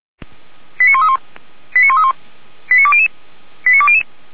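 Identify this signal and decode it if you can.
Five-tone selective call (ZVEI), heard as quick runs of five tones, tones of about 70 ms. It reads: 89131 89131 89200 89200
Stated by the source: ZVEI five-tone selective call received on an SDR.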